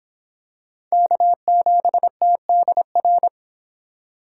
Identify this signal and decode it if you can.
Morse code, heard as a keyed tone, about 25 wpm, 700 Hz. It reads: K7TDR